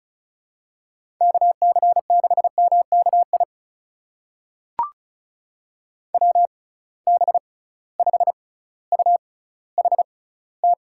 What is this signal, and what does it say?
Morse code, heard as a keyed tone, about 34 wpm, spaced as for 18 wpm, 700 Hz. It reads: KC6MKI WB5UHT
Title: USA Call Signs - 6 Characters - 6x Character Spacing - 35wpm